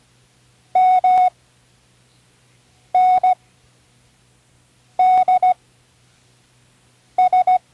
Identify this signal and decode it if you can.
Morse code, heard as a keyed tone, about 17 wpm, spaced as for 5 wpm, 740 Hz. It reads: MNDS
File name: CWResources / Learning Morse Code (from Tune in the World) / 10 random